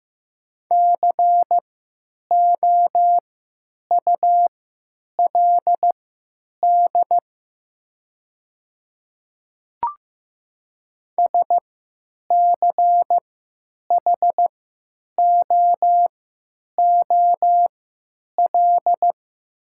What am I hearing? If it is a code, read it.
Morse code, heard as a keyed tone, about 15 wpm, spaced as for 9 wpm, 700 Hz. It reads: COULD SCHOOL
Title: 3x Character Spacing - Top 300 Words - 15wpm